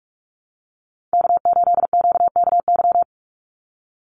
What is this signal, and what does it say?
Morse code, heard as a keyed tone, about 44 wpm, 700 Hz. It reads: K8QKY
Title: Call Signs Repeated 2x - 45wpm